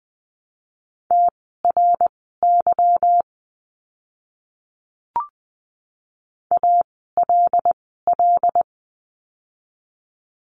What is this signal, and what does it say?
Morse code, heard as a keyed tone, about 20 wpm, spaced as for 15 wpm, 700 Hz. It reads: TRY ALL
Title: 160 Lesson 26f - Letter g - Words 20wpm